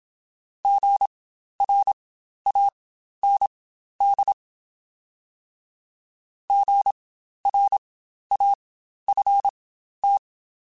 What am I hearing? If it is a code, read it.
Morse code, heard as a keyed tone, about 27 wpm, spaced as for 12 wpm, 800 Hz. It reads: GRAND GRAFT
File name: Spaced Alphas / Spaced Alphas mp3s / Gs sp @25